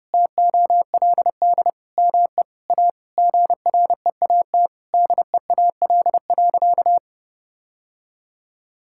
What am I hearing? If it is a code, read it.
Morse code, heard as a keyed tone, about 32 wpm, 700 Hz. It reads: TOLD ME A GREAT DEAL.
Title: Sentences from Top 1000 Words - Review Part 6 - 30wpm